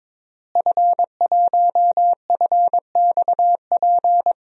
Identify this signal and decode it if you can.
Morse code, heard as a keyed tone, about 21 wpm, 700 Hz. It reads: F1FXP
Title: DXCC-100 Call Signs - 5 Characters - Repeated 3x - 22wpm